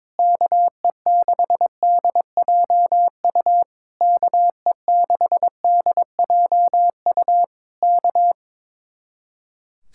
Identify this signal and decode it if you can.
Morse code, heard as a keyed tone, about 22 wpm, 700 Hz. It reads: KE6DJU KE6DJU K
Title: Speed-Racing Common QSO Phrases - 22wpm